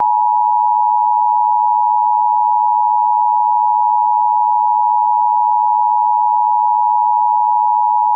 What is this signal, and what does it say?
Emergency Alert System (SAME) tones (the 853 and 960 Hz attention tone). It is heard from 0.0 to 8.2 s.